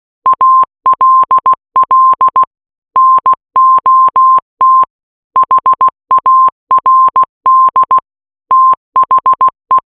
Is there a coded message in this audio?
Yes: Morse code from 0.3 to 9.8 s, heard as a keyed tone, about 16 wpm, 1050 Hz. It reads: ALL NOT HARD THE